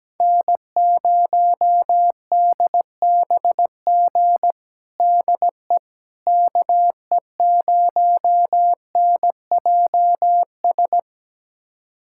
Morse code, heard as a keyed tone, about 17 wpm, 700 Hz. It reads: N0DBG DE KE0NJS
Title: USA Callsign DE Callsign 17wpm